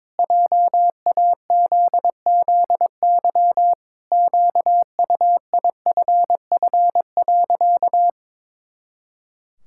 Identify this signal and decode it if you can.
Morse code, heard as a keyed tone, about 22 wpm, 700 Hz. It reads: JAZZY QUIFF.
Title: Pangram Challenge v2 - 22wpm